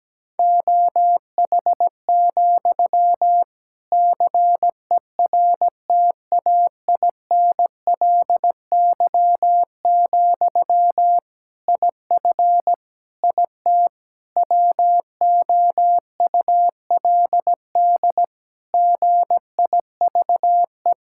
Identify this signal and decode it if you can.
Morse code, heard as a keyed tone, about 17 wpm, 700 Hz. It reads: OH, CERTAINLY, IF IT WOULD GIVE
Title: Sentences from Top 1600 Words 17wpm